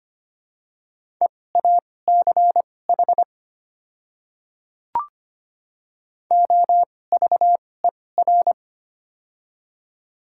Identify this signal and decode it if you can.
Morse code, heard as a keyed tone, about 25 wpm, spaced as for 18 wpm, 700 Hz. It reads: EACH OVER
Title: 2x Character Spacing - Top 100 Words - 25wpm